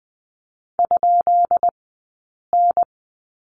Morse code, heard as a keyed tone, about 20 wpm, spaced as for 8 wpm, 700 Hz. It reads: ?N
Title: 171 Lesson 28d - Number 9 - Two Characters 20wpm